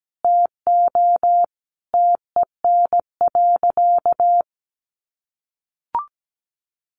Morse code, heard as a keyed tone, about 17 wpm, 700 Hz. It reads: TO TEN.
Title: Easy Phrases 17wpm